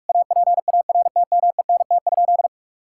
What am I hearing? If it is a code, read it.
Morse code, heard as a keyed tone, about 41 wpm, 700 Hz. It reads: APARTMENT?
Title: Sentences from Top 1000 Words - Review Part 6 - 45wpm